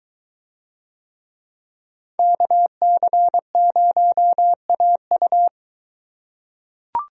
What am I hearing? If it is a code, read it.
Morse code, heard as a keyed tone, about 22 wpm, 700 Hz. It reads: KC0AU